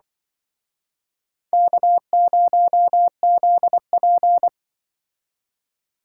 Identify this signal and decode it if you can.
Morse code, heard as a keyed tone, about 24 wpm, 700 Hz. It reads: K0ZP
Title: Speed Racing - USA 1x2 Call Signs 20wpm